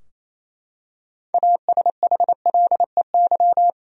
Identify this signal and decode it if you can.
Morse code, heard as a keyed tone, about 28 wpm, 700 Hz. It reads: ASHLEY
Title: USA Common Names - 28wpm